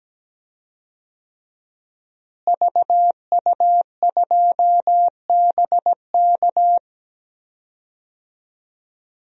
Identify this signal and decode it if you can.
Morse code, heard as a keyed tone, about 17 wpm, 700 Hz. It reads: VU2BK
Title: DXCC-100 Call Signs - 5 Characters - Repeated 3x - 17wpm